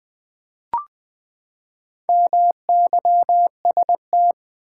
Morse code, heard as a keyed tone, about 20 wpm, 700 Hz. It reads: MYST
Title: Build A Word - 4 Letters At A Time - 10-Letter Words - 20wpm